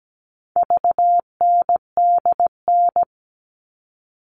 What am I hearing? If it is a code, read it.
Morse code, heard as a keyed tone, about 17 wpm, 700 Hz. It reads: VNDN
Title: All Cut Numbers - 4-Digit 17wpm